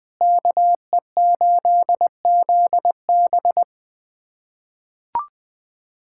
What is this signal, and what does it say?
Morse code, heard as a keyed tone, about 21 wpm, 700 Hz. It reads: KE8ZB
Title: USA Callsign DE Callsign - 5 Characters - 20wpm